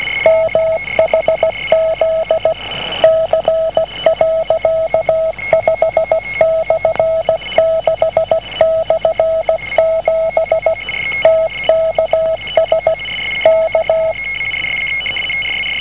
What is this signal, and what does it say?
Morse code, heard as a keyed tone, about 17 wpm, 650 Hz. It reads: MHZ C.5/6/7 TKS K